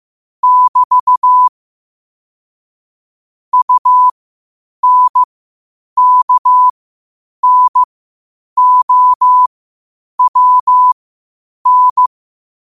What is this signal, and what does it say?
Morse code, heard as a keyed tone, about 15 wpm, spaced as for 8 wpm, 1000 Hz. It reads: <BT> UNKNOWN
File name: qotdfiles / 8wpm